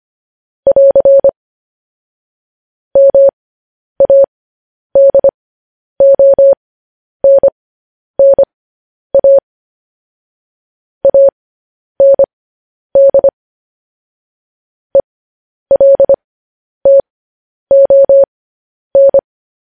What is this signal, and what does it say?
Morse code, heard as a keyed tone, about 25 wpm, spaced as for 10 wpm, 550 Hz. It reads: <AR> MADONNA AND ELTON
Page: news headlines in morse code at 10 wpm